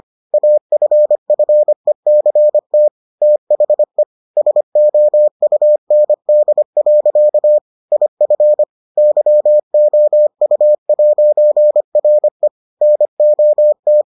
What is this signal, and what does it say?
Morse code, heard as a keyed tone, about 25 wpm, 600 Hz. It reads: AFFECT THE SOUND. IF YOU'RE NOT